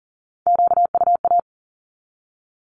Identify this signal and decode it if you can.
Morse code, heard as a keyed tone, about 40 wpm, 700 Hz. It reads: QUA
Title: Q-Codes 40wpm